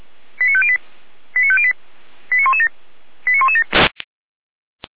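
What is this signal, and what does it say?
Five-tone selective call (ZVEI), heard as quick runs of five tones, tones of about 70 ms. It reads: 89508 89508 89117 89117
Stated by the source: ZVEI five-tone selective call received with an SDR.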